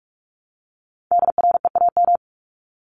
Morse code, heard as a keyed tone, about 44 wpm, 700 Hz. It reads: DREAM
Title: Top 1251-1500 Words 45wpm